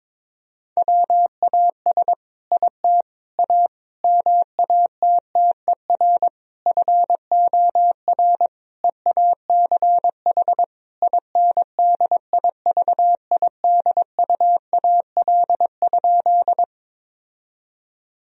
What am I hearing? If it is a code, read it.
Morse code, heard as a keyed tone, about 22 wpm, 700 Hz. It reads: WAS IT A MATTER FOR EACH INDIVIDUAL?
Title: Sentences from Top 1000 Words - Review Part 4 - 22wpm